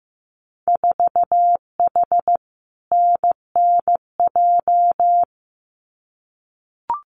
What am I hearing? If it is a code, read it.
Morse code, heard as a keyed tone, about 15 wpm, 700 Hz. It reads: 4H NNJ
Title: Winter Field Day Exchanges Encore 15wpm